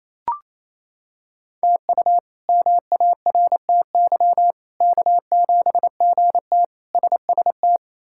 Morse code, heard as a keyed tone, about 28 wpm, 700 Hz. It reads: TU MARTY K7GT SST
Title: K1USN SST QSO Practice - 28wpm